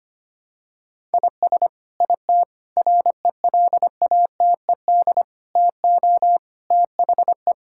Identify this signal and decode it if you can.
Morse code, heard as a keyed tone, about 25 wpm, 700 Hz. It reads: IS IT RELATED TO THE
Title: Sentences from Top 1000 Words - Review Part 5 - 25wpm